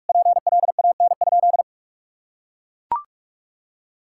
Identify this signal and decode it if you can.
Morse code, heard as a keyed tone, about 41 wpm, 700 Hz. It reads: PLAN?
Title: Sentences from Top 1000 Words - Review Part 6 - 45wpm